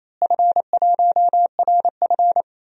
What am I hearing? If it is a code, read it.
Morse code, heard as a keyed tone, about 28 wpm, 700 Hz. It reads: F1RF
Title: DXCC-100 Call Signs - 4 Characters - Repeated 3x - 28wpm